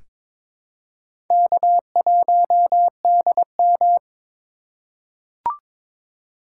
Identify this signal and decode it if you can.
Morse code, heard as a keyed tone, about 23 wpm, 700 Hz. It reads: K1DM
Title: K1USN SST QSO Practice - 22wpm